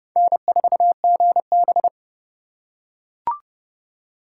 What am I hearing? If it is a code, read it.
Morse code, heard as a keyed tone, about 26 wpm, 700 Hz. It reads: N4GB